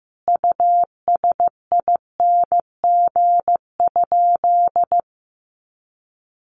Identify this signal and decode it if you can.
Morse code, heard as a keyed tone, about 15 wpm, 700 Hz. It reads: USING?
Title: Common QSO Phrases 15wpm